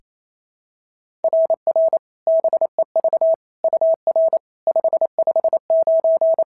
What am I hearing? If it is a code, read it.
Morse code, heard as a keyed tone, about 25 wpm, 650 Hz. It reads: RR BEV UR 559
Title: SOTA QSO Practice 28wpm